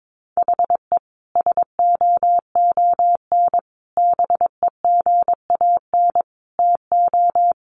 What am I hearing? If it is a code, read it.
Morse code, heard as a keyed tone, about 22 wpm, 700 Hz. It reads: HE SOON BEGAN TO
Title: Sentences from Top 1100 Words 22wpm